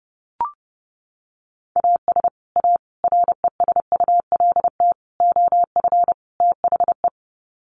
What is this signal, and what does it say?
Morse code, heard as a keyed tone, about 30 wpm, 700 Hz. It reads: AS A RESULT OF THE
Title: Sets of 5 Words 30wpm